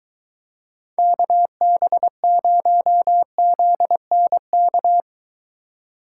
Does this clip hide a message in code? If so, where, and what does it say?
Yes: Morse code from 1.0 to 5.0 s, heard as a keyed tone, about 22 wpm, 700 Hz. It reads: KB0ZNK